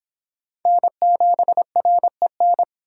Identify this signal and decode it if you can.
Morse code, heard as a keyed tone, about 26 wpm, 700 Hz. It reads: N7REN